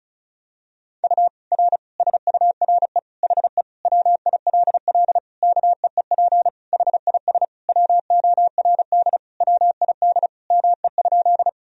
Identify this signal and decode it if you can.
Morse code, heard as a keyed tone, about 36 wpm, 700 Hz. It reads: U R SURE HE WILL KEEP HIS WORD WID ME?
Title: Sentences from Top 300 Words and Abbreviations 35wpm